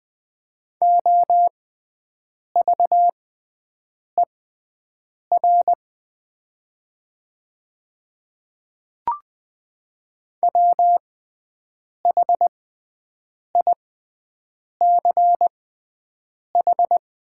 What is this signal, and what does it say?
Morse code, heard as a keyed tone, about 20 wpm, spaced as for 7 wpm, 700 Hz. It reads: OVER WHICH
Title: 6x Character Spacing - Top 100 Words - 20wpm